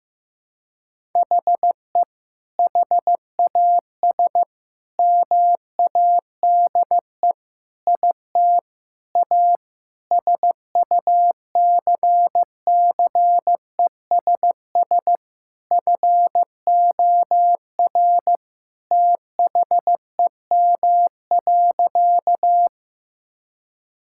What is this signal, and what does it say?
Morse code, heard as a keyed tone, about 15 wpm, 700 Hz. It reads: HE HAS MADE IT A SUCCESS FOR THEM.
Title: Sentences from Top 1000 Words - Review Part 6 - 15wpm